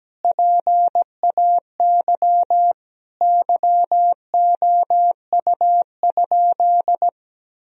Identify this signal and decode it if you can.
Morse code, heard as a keyed tone, about 17 wpm, 700 Hz. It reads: PAY YOU?
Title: Sentences from Top 1000 Words - Review Part 3 - 17wpm